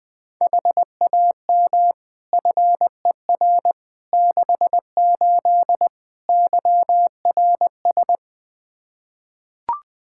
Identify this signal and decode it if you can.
Morse code, heard as a keyed tone, about 20 wpm, 700 Hz. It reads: HAM FER 68 YRS
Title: K1USN SST QSO Practice 20wpm